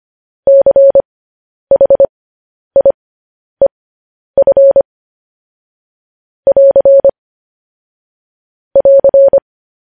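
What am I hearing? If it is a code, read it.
Morse code, heard as a keyed tone, about 25 wpm, spaced as for 10 wpm, 550 Hz. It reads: CHIEF <AR> <AR>
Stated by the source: Morse code news headlines sent at 10 words per minute.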